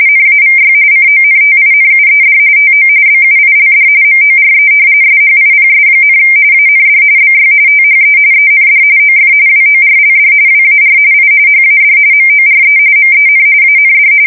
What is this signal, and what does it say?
RTTY, about 45.45 baud. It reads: NAVY CONTINUED TO BUILD MORE SHIPS, AND BY THE END OF WORLD WAR I HAD MORE MEN AND
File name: fsk170 / usnavy1